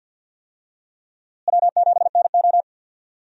Morse code, heard as a keyed tone, about 51 wpm, 700 Hz. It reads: W7NO
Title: USA Call Signs - 4 Characters - Repeated 2x - 50wpm